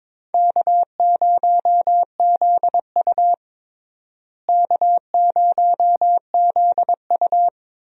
Morse code, heard as a keyed tone, about 22 wpm, 700 Hz. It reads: K0ZU K0ZU
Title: USA Call Signs - 4 Characters - Repeated 2x - 22wpm